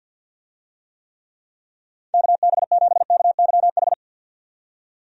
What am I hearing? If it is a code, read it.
Morse code, heard as a keyed tone, about 51 wpm, 700 Hz. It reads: KB7KYH
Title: USA Call Signs - 6 Characters - 50wpm